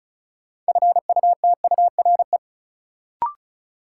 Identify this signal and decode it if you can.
Morse code, heard as a keyed tone, about 36 wpm, 700 Hz. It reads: FUTURE
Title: Sentences from Top 700 Words - New Word Drill - 35wpm